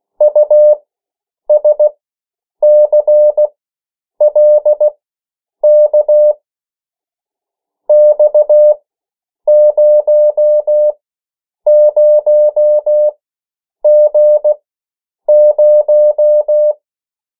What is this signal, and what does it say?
Morse code, heard as a keyed tone, about 16 wpm, spaced as for 8 wpm, 600 Hz. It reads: USCLK X00G0
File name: morse mp3 / 14 urok 13